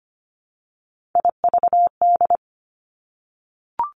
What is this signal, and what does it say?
Morse code, heard as a keyed tone, about 25 wpm, 700 Hz. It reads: IVD
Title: Three Challenging Letter-Number Combinations 25wpm